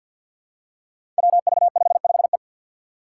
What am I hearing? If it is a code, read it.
Morse code, heard as a keyed tone, about 49 wpm, 700 Hz. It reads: WV55E
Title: Five Challenging Letter-Number Combinations 50wpm